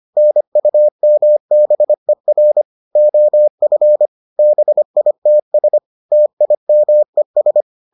Morse code, heard as a keyed tone, about 25 wpm, 600 Hz. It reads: NUMBER OF BITS TIMES